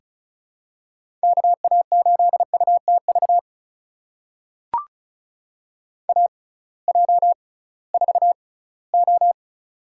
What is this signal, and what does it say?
Morse code, heard as a keyed tone, about 34 wpm, spaced as for 20 wpm, 700 Hz. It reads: KA8UTV AJ4O
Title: USA Call Signs - 6x Character Spacing - 35wpm